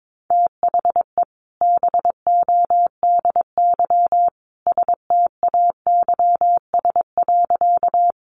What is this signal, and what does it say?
Morse code, heard as a keyed tone, about 22 wpm, 700 Hz. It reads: THE BODY STAYS.
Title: A Sound Of Thunder 22wpm